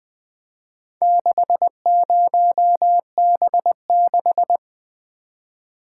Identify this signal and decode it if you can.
Morse code, heard as a keyed tone, about 20 wpm, 700 Hz. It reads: 60B6
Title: Four Letter-Number Combinations - 20wpm